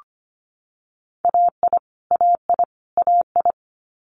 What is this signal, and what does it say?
Morse code, heard as a keyed tone, about 25 wpm, 700 Hz. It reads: AI AI AI
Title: Multi-letter Phonemes Repeated 3x 25wpm